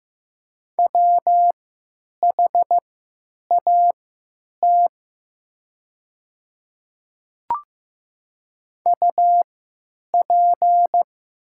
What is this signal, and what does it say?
Morse code, heard as a keyed tone, about 15 wpm, spaced as for 9 wpm, 690 Hz. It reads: WHAT UP
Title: 3x Character Spacing - Top 100 Words - 15wpm